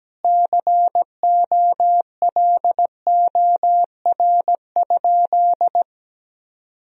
Morse code, heard as a keyed tone, about 17 wpm, 700 Hz. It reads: COLOR?
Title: Sentences from Top 1000 Words - Review Part 2 - 17wpm